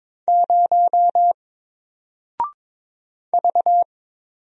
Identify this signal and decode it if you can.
Morse code, heard as a keyed tone, about 22 wpm, 700 Hz. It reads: 0 V
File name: Single-Letter-Number-22wpm